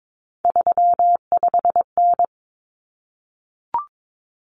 Morse code, heard as a keyed tone, about 22 wpm, 700 Hz. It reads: 35N
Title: Common Cut Numbers - 3-Digit 22wpm